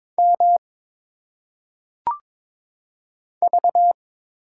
Morse code, heard as a keyed tone, about 22 wpm, 700 Hz. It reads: M V
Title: Single Letters - Phonetic Pronunciation 22wpm